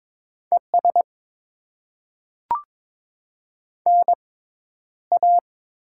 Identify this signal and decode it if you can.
Morse code, heard as a keyed tone, about 22 wpm, 690 Hz. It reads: E S N A
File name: Top 20 QSO Elements - 6x Character Spacing - 22wpm